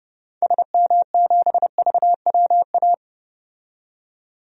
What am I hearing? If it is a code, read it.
Morse code, heard as a keyed tone, about 30 wpm, 700 Hz. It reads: SM7VWA
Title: DXCC-100 Call Signs - 6 Characters - 30wpm